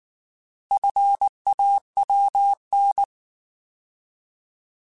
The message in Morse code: FAWN

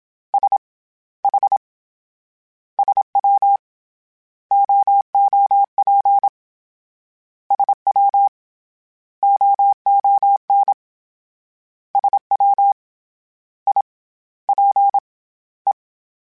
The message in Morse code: S H SW OOP SW OON SW I P E

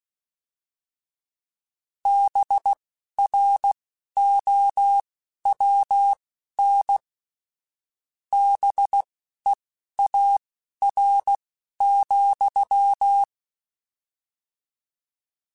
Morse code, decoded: BROWN BEAR,

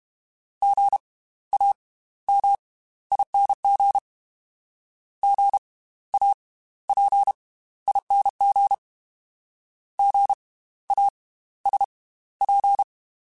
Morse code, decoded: GAMING GAPING GASP

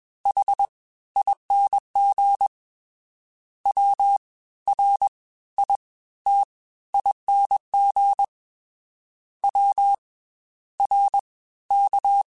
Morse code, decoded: HING WRITING WRK

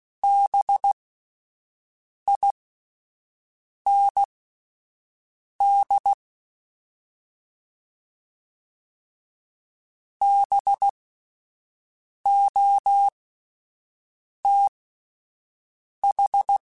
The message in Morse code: BIND BOTH

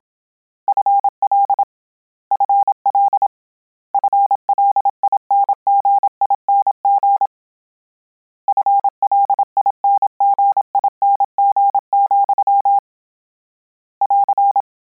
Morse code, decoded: FL FL FLINGING FLINGING, <AR>

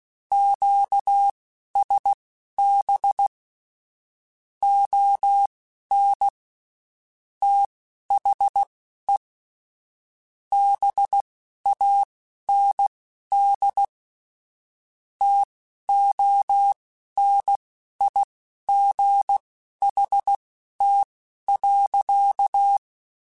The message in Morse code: QSB ON THE BAND TONIGHT.

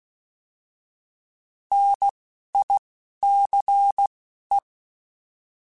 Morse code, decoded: NICE

